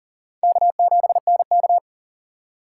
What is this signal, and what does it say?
Morse code, heard as a keyed tone, about 40 wpm, 700 Hz. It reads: K7NK
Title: USA Call Signs - 4 Characters - 2x Character Spacing - 40wpm